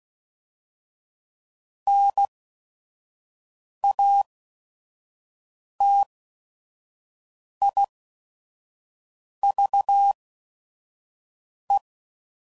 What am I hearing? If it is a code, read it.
Morse code, heard as a keyed tone, about 16 wpm, spaced as for 5 wpm, 800 Hz. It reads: NATIVE
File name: Prefixes / By First Letter / Ns